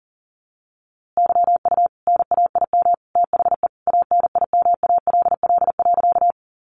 Morse code, heard as a keyed tone, about 37 wpm, 700 Hz. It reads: YU NAIM THE ANIMALL.
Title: A Sound Of Thunder 40wpm